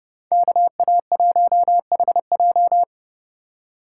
Morse code, heard as a keyed tone, about 35 wpm, 700 Hz. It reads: KA1HJ
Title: Speed Racing - USA 2x2 Call Signs 30wpm